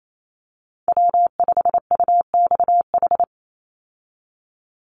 Morse code, heard as a keyed tone, about 28 wpm, 700 Hz. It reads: W5UXH
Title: Call Signs Repeated 2x - 28wpm